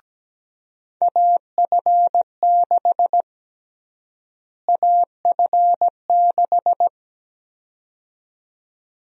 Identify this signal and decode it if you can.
Morse code, heard as a keyed tone, about 17 wpm, 700 Hz. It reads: AF6 AF6
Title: USA Call Sign Prefixes - Repeated 2x - 17wpm